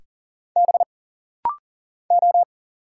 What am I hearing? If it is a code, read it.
Morse code, heard as a keyed tone, about 40 wpm, 700 Hz. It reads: B O
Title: Single Letters - Rapid-Fire 40wpm